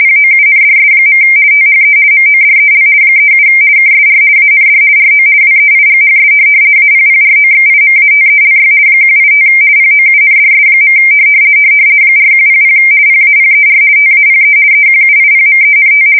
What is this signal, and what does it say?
RTTY, about 50 baud. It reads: POUNDS. THE MODEL 37 KSR AND RO WEIGHS APPROXIMATELY 185 POUNDS. 4100 PAPER TAPE EQUIPMENT
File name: fsk170-50B / teletype-corp